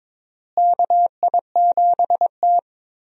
Morse code, heard as a keyed tone, about 22 wpm, 700 Hz. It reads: KI7T
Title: USA Call Signs - 4 Characters - Repeated 3x - 22wpm